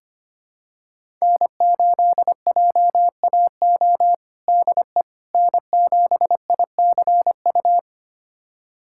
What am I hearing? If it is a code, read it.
Morse code, heard as a keyed tone, about 25 wpm, 700 Hz. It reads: N8JAO DE N7ICU
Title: USA Callsign DE Callsign 25wpm